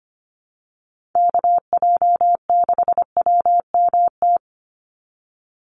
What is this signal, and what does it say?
Morse code, heard as a keyed tone, about 25 wpm, 700 Hz. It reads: KJ6WMT
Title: USA Call Signs 25wpm